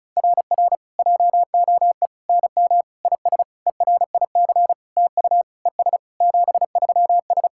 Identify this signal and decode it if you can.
Morse code, heard as a keyed tone, about 36 wpm, 700 Hz. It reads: RR JOE NM IS ERIC TU ES 73S
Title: SOTA QSO Practice 35wpm